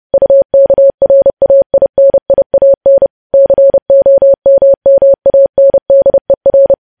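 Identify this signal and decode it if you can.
Morse code, heard as a keyed tone, about 30 wpm, 550 Hz. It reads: UKRAINIAN COMMANDER
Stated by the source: Morse code practice of news headlines at 30 wpm, preceded by vvv.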